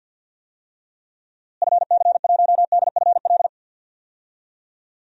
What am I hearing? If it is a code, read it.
Morse code, heard as a keyed tone, about 48 wpm, 700 Hz. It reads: FC1DFL
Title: DXCC-100 Call Signs - 6 Characters - Repeated 2x - 50wpm